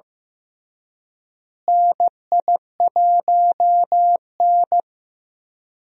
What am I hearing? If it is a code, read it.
Morse code, heard as a keyed tone, about 16 wpm, 700 Hz. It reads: NI1N
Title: USA Call Signs - 4 Characters - Repeated 3x - 15wpm